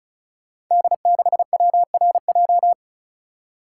Morse code, heard as a keyed tone, about 36 wpm, 700 Hz. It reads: D6WRJ
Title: Code Groups - 5 Letter:Numbers - 35wpm